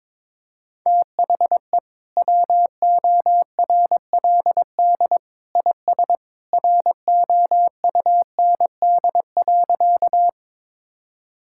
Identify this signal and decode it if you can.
Morse code, heard as a keyed tone, about 22 wpm, 700 Hz. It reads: THE WORLD IS ROUND.
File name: practice / 22wpm / Sentences with 2-Word Phrases - 22wpm